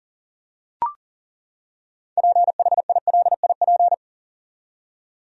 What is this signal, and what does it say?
Morse code, heard as a keyed tone, about 40 wpm, 700 Hz. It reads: PHILIP
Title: USA Common Names - 40wpm